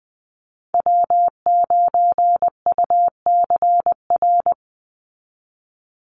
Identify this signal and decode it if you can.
Morse code, heard as a keyed tone, about 20 wpm, 700 Hz. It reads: W9UCR
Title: Call Signs Repeated 3x - 20wpm